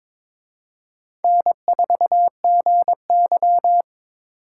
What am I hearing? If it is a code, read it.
Morse code, heard as a keyed tone, about 26 wpm, 700 Hz. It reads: N4GY
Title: Speed Racing - USA 1x2 Call Signs 22wpm